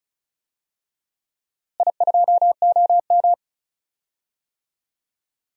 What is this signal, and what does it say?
Morse code, heard as a keyed tone, about 36 wpm, 700 Hz. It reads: I2OM
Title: DXCC-100 Call Signs - 4 Characters - Repeated 3x - 35wpm